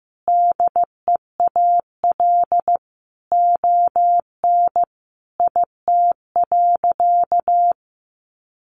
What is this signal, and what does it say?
Morse code, heard as a keyed tone, about 15 wpm, 700 Hz. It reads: DEAL ON IT.
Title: Sentences from Top 1300 Words 15wpm